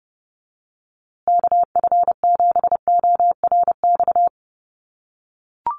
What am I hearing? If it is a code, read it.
Morse code, heard as a keyed tone, about 30 wpm, 700 Hz. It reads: KF7ORX
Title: USA Call Signs Repeated 3x 30wpm